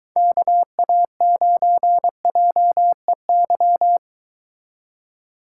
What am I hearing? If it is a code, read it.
Morse code, heard as a keyed tone, about 24 wpm, 700 Hz. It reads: KA9JEY